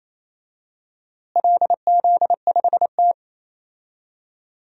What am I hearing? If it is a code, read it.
Morse code, heard as a keyed tone, about 28 wpm, 700 Hz. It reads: LZ5T
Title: DXCC-100 Call Signs - 4 Characters - Repeated 3x - 28wpm